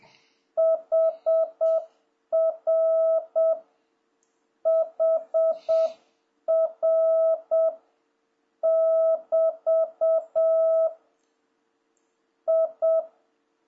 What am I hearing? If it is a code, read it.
Morse code, heard as a keyed tone, about 7 wpm, 660 Hz. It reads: HR HR <BT> I
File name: jh8cbh / 2026cw